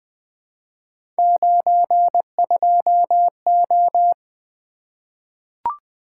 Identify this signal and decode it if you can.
Morse code, heard as a keyed tone, about 20 wpm, 700 Hz. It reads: 92O
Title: Three Letter-Number Combinations - 20wpm